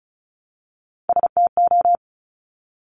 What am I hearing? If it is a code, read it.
Morse code, heard as a keyed tone, about 36 wpm, 700 Hz. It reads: STO
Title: Top Three Letter Combinations in Words 35wpm